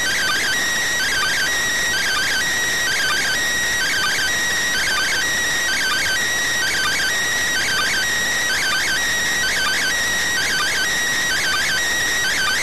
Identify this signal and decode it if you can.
Two-tone teleprinter signal, about 400 Hz shift.